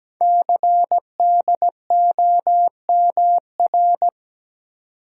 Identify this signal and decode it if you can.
Morse code, heard as a keyed tone, about 17 wpm, 700 Hz. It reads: CDOMR
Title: Code Groups - 5 Letters - 17wpm